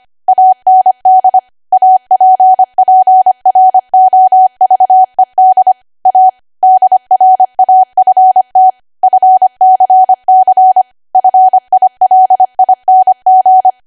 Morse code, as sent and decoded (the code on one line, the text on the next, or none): .- -. -.. / .- .--. .--. .-. --- ...- . -.. / .- / -.. .-. .- ..-. - / ..-. -.-. -.-. / ..-. .. .-.. .. -. --.
AND APPROVED A DRAFT FCC FILING